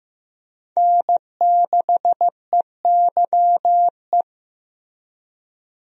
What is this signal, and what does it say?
Morse code, heard as a keyed tone, about 15 wpm, 700 Hz. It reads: N6EYE
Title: USA Call Signs - 5 Characters - 15wpm